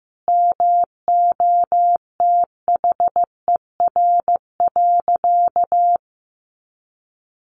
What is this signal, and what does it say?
Morse code, heard as a keyed tone, about 15 wpm, 700 Hz. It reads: MOTHER.
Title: Sentences from Top 1300 Words 15wpm